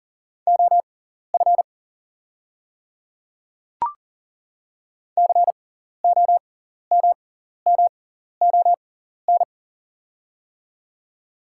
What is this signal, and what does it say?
Morse code, heard as a keyed tone, about 39 wpm, spaced as for 14 wpm, 700 Hz. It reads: OF COMMON
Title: 6x Character Spacing - Top 500 Words - 40wpm